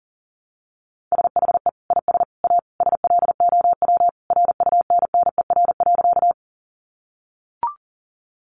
Morse code, heard as a keyed tone, about 40 wpm, 700 Hz. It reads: SHE IS A SLOW RUNNER.
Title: Additional Easy Phrases 40wpm